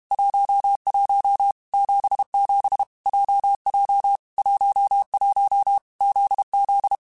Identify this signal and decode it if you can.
Morse code, heard as a keyed tone, about 32 wpm, 800 Hz. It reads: 11 77 JJ 11 ZZ
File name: Basic Letter Groups By7Groups / JZ17 / JZ17@30